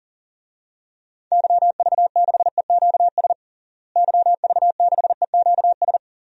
Morse code, heard as a keyed tone, about 40 wpm, 700 Hz. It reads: YV6EQS YV6EQS
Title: DXCC-100 Call Signs - 6 Characters - Repeated 3x - 40wpm